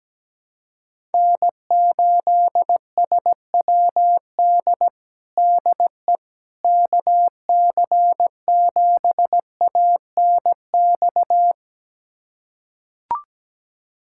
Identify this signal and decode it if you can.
Morse code, heard as a keyed tone, about 17 wpm, 700 Hz. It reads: N8SWD DE KC7ANX
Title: USA Callsign DE Callsign 17wpm